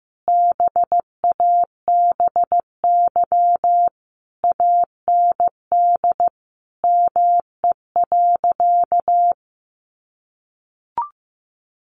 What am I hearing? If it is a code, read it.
Morse code, heard as a keyed tone, about 15 wpm, 700 Hz. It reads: BABY AND ME.
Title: Sentences from Top 1300 Words 15wpm